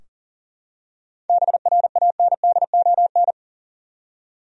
Morse code, heard as a keyed tone, about 40 wpm, 700 Hz. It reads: BRANDON
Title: USA Common Names - 40wpm